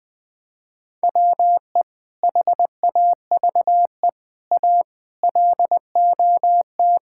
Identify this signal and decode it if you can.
Morse code, heard as a keyed tone, about 20 wpm, 700 Hz. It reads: WE HAVE A LOT
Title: Sets of 8 Words - 20wpm